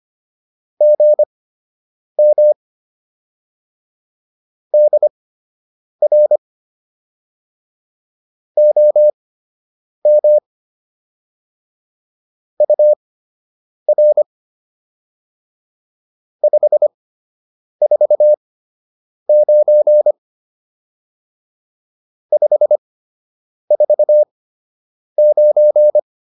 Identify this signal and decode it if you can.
Morse code, heard as a keyed tone, about 25 wpm, spaced as for 8 wpm, 600 Hz. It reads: GM DR OM UR 549 549